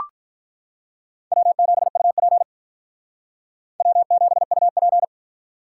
Morse code, heard as a keyed tone, about 44 wpm, 700 Hz. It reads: W7UP W7UP